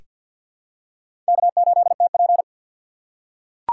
Morse code, heard as a keyed tone, about 48 wpm, 700 Hz. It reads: K8TP
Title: USA Call Signs - 4 Characters - 3x Character Spacing - 50wpm